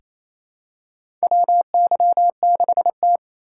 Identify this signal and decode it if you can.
Morse code, heard as a keyed tone, about 29 wpm, 700 Hz. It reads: WY6T
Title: USA Call Signs - 4 Characters - 2x Character Spacing - 28wpm